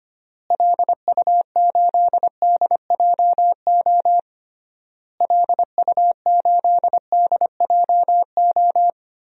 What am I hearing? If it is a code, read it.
Morse code, heard as a keyed tone, about 25 wpm, 700 Hz. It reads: LU8DJO LU8DJO
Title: DXCC-100 Call Signs - 6 Characters - Repeated 3x - 25wpm